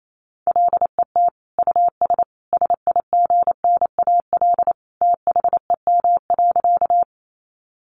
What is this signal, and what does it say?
Morse code, heard as a keyed tone, about 28 wpm, 700 Hz. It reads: LET US SIGNAL THEM.
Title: Sentences from Top 1400 Words 28wpm